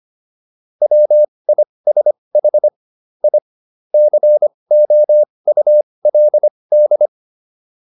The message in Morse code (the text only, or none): WISH I COULD